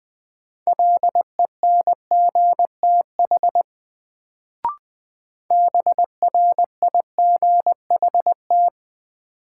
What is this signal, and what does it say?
Morse code, heard as a keyed tone, about 20 wpm, 700 Hz. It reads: LENGTH BRIGHT
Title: Sentences from Top 1000 Words - New Word Drill - 20wpm